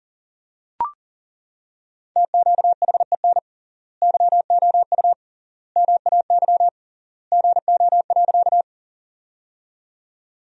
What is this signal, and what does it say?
Morse code, heard as a keyed tone, about 40 wpm, 700 Hz. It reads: TQHEN YOU MAY GO.
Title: 3x Word-Spacing - Sentences from Top 100 Words - 40wpm